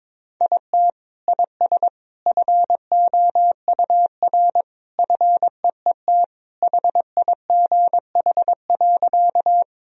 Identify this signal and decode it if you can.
Morse code, heard as a keyed tone, about 21 wpm, 700 Hz. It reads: IT IS FOUR FEET HIGH.
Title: Sentences from Top 1000 Words - Review Part 1 - 22wpm